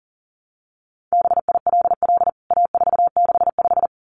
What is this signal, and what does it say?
Morse code, heard as a keyed tone, about 40 wpm, 700 Hz. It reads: BILL A465
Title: CWT Exchange 40wpm